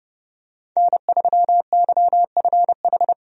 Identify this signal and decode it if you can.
Morse code, heard as a keyed tone, about 30 wpm, 700 Hz. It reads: N3YFH
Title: USA Call Signs - 5 Characters - 30wpm